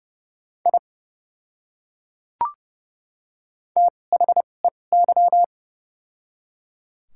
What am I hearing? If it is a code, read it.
Morse code, heard as a keyed tone, about 30 wpm, spaced as for 22 wpm, 700 Hz. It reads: I THEY
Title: 2x Character Spacing - Top 100 Words - 30wpm